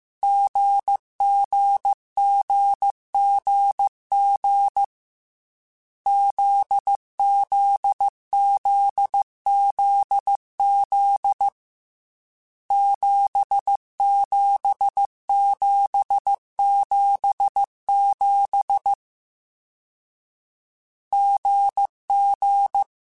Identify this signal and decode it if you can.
Morse code, heard as a keyed tone, about 15 wpm, 800 Hz. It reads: GGGGG ZZZZZ 77777 GG